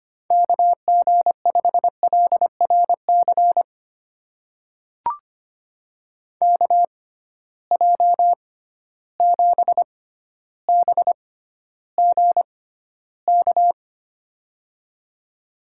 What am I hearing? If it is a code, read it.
Morse code, heard as a keyed tone, about 25 wpm, spaced as for 13 wpm, 700 Hz. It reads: KG5LRC KJ7BGK
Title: USA Call Signs - 6 Characters - 6x Character Spacing - 25wpm